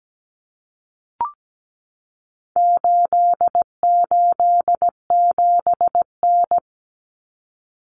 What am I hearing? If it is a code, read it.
Morse code, heard as a keyed tone, about 17 wpm, 700 Hz. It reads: 887N
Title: Common Cut Numbers - 4-Digit 17wpm